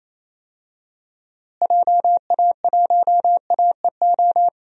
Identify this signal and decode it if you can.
Morse code, heard as a keyed tone, about 28 wpm, 700 Hz. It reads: JA1AEO
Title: DXCC-100 Call Signs - 6 Characters - Repeated 2x - 28wpm